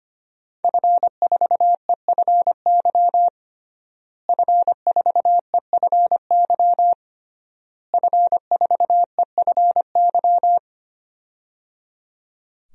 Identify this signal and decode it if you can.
Morse code, heard as a keyed tone, about 25 wpm, 700 Hz. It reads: F4EFY F4EFY F4EFY
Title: DXCC-100 Call Signs - 5 Characters - Repeated 3x - 25wpm